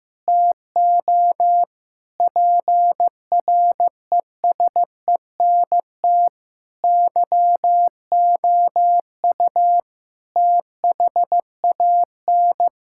Morse code, heard as a keyed tone, about 15 wpm, 700 Hz. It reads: TO PRESENT YOU THAN
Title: Sentences from Top 1000 Words - Review Part 6 - 15wpm